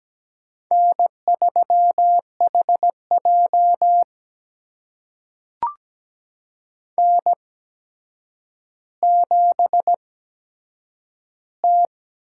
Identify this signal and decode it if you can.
Morse code, heard as a keyed tone, about 17 wpm, spaced as for 8 wpm, 700 Hz. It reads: N3HJ N7T
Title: USA Call Signs - 4 Characters - 8x Character Spacing - 17wpm